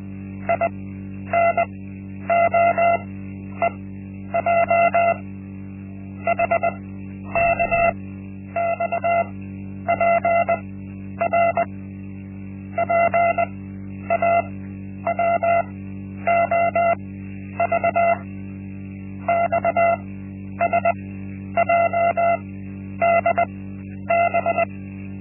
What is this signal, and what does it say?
Morse code, heard as a keyed tone, about 20 wpm, spaced as for 10 wpm, 680 Hz. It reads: INOEJ HKXPR PAWOV XSJDB
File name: Linksmakalnis-1